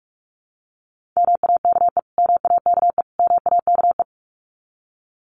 Morse code, heard as a keyed tone, about 44 wpm, 700 Hz. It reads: MAKE MAKE MAKE
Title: Top 500 Words Repeated 45wpm